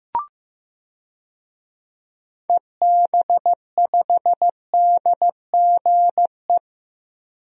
Morse code, heard as a keyed tone, about 15 wpm, 700 Hz. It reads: EB5DGE
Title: DXCC-100 Call Signs - 6 Characters - 15wpm